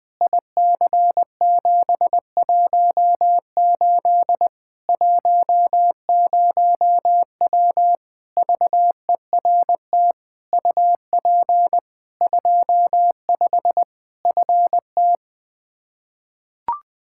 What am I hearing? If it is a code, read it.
Morse code, heard as a keyed tone, about 21 wpm, 700 Hz. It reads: IC718 10W VERT UP 25 FT
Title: Practice QSO - Exchange 2 - Info - 20wpm